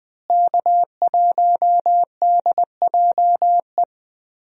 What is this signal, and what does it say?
Morse code, heard as a keyed tone, about 21 wpm, 700 Hz. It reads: K1DJE
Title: USA Call Signs - 5 Characters - Repeated 3x - 20wpm